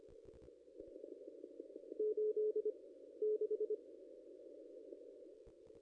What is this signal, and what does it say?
Morse code, heard as a keyed tone, about 24 wpm, 430 Hz. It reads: 8 6
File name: contests / ss-cw-k5tr-op-n5rz -1800